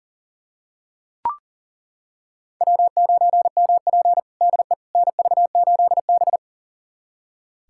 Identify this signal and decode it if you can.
Morse code, heard as a keyed tone, about 37 wpm, 700 Hz. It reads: W9MP DE NV8B
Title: USA Callsign DE Callsign - 4 Characters - 40wpm